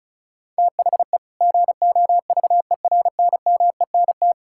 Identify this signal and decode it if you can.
Morse code, heard as a keyed tone, about 36 wpm, 700 Hz. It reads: THE GOVERNMENT
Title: Sentences from Top 1600 Words 35wpm